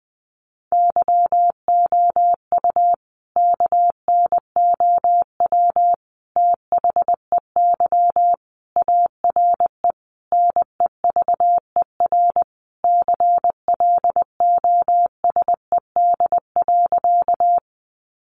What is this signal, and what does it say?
Morse code, heard as a keyed tone, about 20 wpm, 700 Hz. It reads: YOU KNOW THEY ARE NEVER CLOSED.
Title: Sentences from Top 1400 Words 20wpm